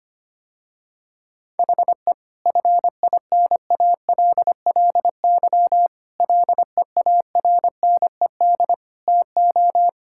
Morse code, heard as a keyed tone, about 25 wpm, 700 Hz. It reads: HE FINALLY LEARNED TO
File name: practice / 25wpm / Difficult Sentences with Hints - 25wpm